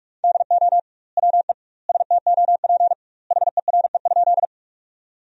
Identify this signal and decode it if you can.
Morse code, heard as a keyed tone, about 47 wpm, 700 Hz. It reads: DO WE STOP HERE?
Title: 2x Word-Spacing - Sentences from Top 300 Words - 45wpm